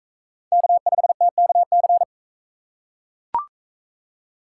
Morse code, heard as a keyed tone, about 40 wpm, 700 Hz. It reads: K5TKC